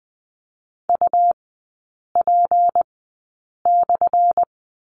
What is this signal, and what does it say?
Morse code, heard as a keyed tone, about 20 wpm, spaced as for 8 wpm, 700 Hz. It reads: UP/
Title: 178 Lesson 29e - Slash - Three Characters 20wpm